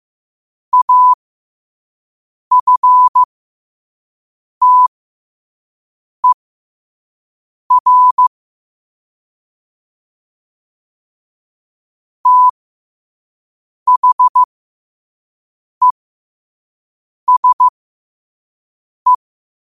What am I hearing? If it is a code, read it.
Morse code, heard as a keyed tone, about 15 wpm, spaced as for 5 wpm, 1000 Hz. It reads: AFTER THESE
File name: qotdfiles / 5wpm